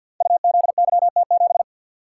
Morse code, heard as a keyed tone, about 49 wpm, 700 Hz. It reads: UZYT7
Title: Code Groups - 5 Letter:Numbers - 50wpm